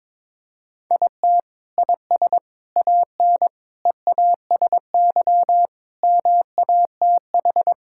Morse code, heard as a keyed tone, about 22 wpm, 700 Hz. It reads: IT IS AN EASY MATH
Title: Sentences from Top 1000 Words - Review Part 6 - 22wpm